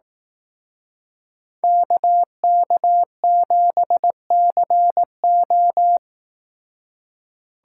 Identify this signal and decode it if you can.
Morse code, heard as a keyed tone, about 18 wpm, 700 Hz. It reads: KK7CO
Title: Speed Racing - USA 2x2 Call Signs 15wpm